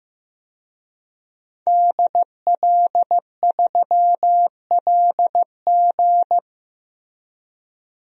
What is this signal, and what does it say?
Morse code, heard as a keyed tone, about 16 wpm, 700 Hz. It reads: DL3LG
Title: DXCC-100 Call Signs - 5 Characters - 15wpm